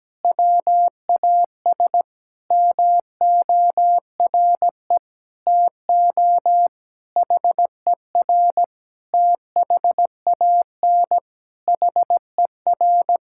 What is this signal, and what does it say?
Morse code, heard as a keyed tone, about 17 wpm, 700 Hz. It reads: WAS MORE TO HER THAN HER
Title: Sentences from Top 1000 Words - Review Part 6 - 17wpm